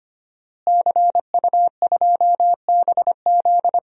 Morse code, heard as a keyed tone, about 25 wpm, 700 Hz. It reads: CU2BZ
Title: DXCC-100 Call Signs - 5 Characters - Repeated 2x - 25wpm